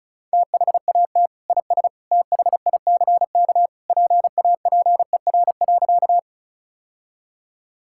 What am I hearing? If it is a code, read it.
Morse code, heard as a keyed tone, about 36 wpm, 700 Hz. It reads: THAT IS THICK PAPER.
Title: Sentences from Top 1600 Words 35wpm